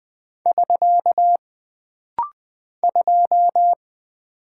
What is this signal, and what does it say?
Morse code, heard as a keyed tone, about 20 wpm, 700 Hz. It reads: <SK> 2
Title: Single Character (Letter:Number:Punctuation:Prosign) - Rapid-Fire - 20wpm